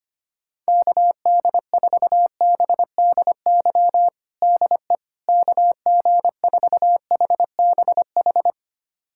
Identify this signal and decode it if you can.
Morse code, heard as a keyed tone, about 25 wpm, 700 Hz. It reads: KD4BDY DE KG4HBH
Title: USA Callsign DE Callsign 25wpm